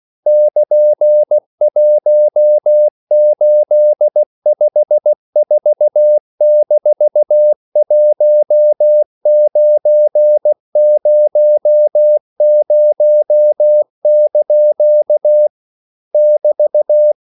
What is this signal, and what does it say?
Morse code, heard as a keyed tone, about 16 wpm, 600 Hz. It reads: <KN>1854-1900) <BT>